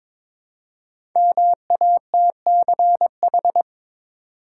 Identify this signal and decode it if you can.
Morse code, heard as a keyed tone, about 22 wpm, 700 Hz. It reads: MATCH
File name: 5 Letter Words - First 3 Letters Given - 22wpm